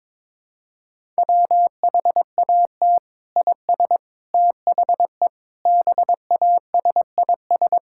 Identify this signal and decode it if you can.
Morse code, heard as a keyed tone, about 22 wpm, 700 Hz. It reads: WHAT IS THE BASIS
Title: Sentences from Top 1000 Words - Review Part 5 - 22wpm